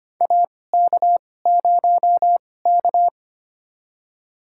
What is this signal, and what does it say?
Morse code, heard as a keyed tone, about 25 wpm, spaced as for 18 wpm, 700 Hz. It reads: AK0K
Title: USA Call Signs - 4 Characters - 2x Character Spacing - 25wpm